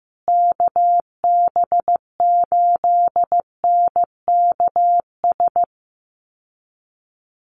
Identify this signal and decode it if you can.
Morse code, heard as a keyed tone, about 15 wpm, 700 Hz. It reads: KB8NKS
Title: USA Call Signs Repeated 3x 15wpm